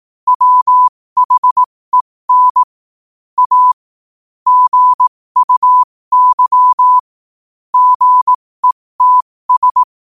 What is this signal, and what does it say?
Morse code, heard as a keyed tone, about 18 wpm, spaced as for 15 wpm, 1000 Hz. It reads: WHEN A GUY GETS
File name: qotdfiles / 15wpm